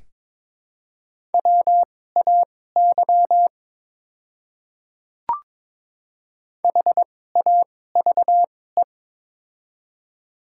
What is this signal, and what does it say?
Morse code, heard as a keyed tone, about 22 wpm, spaced as for 16 wpm, 700 Hz. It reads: WAY HAVE
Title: 2x Character Spacing - Top 100 Words - 22wpm